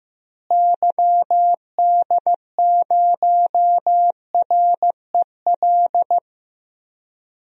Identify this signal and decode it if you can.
Morse code, heard as a keyed tone, about 15 wpm, 700 Hz. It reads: YD0REL
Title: DXCC-100 Call Signs - 6 Characters - Repeated 2x - 15wpm